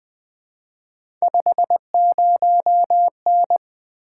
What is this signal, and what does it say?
Morse code, heard as a keyed tone, about 20 wpm, 700 Hz. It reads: 50N
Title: Three Letter-Number Combinations - 20wpm